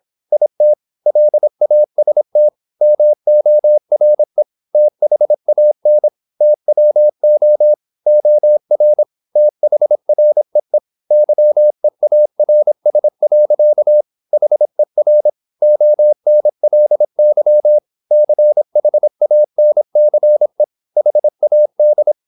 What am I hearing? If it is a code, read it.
Morse code, heard as a keyed tone, about 26 wpm, 600 Hz. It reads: IT LAST MORE THAN TWO OR THREE YEARS. HER ONLY CHANCE HAD
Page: Never 26wpm | CW med Gnister